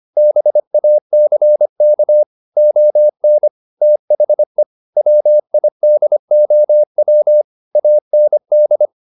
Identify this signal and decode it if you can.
Morse code, heard as a keyed tone, about 25 wpm, 600 Hz. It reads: BACK ON THE WIDOW AND